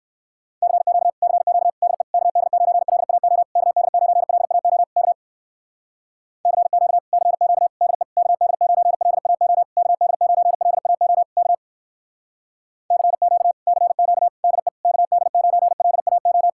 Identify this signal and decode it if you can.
Morse code, heard as a keyed tone, about 57 wpm, 700 Hz. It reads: CQ CQ DE KD9LAO KD9LAO K CQ CQ DE KD9LAO KD9LAO K CQ CQ DE KD9LAO